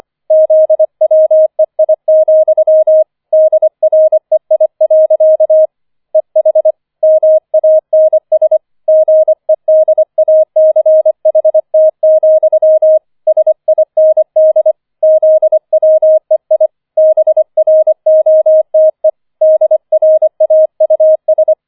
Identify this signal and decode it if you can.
Morse code, heard as a keyed tone, about 25 wpm, 630 Hz. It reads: ZWEI, DREI. EH MANS GEDACHT, SIND ZWEI BROTE DRAUS
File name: 25wpm